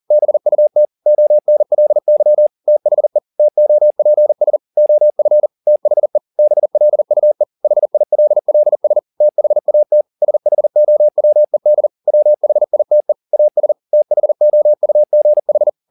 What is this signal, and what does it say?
Morse code, heard as a keyed tone, about 40 wpm, 600 Hz. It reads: BUT ONLY THE TOPS OF THE BLUE HILLS THAT SHOWED WHITE AS THOUGH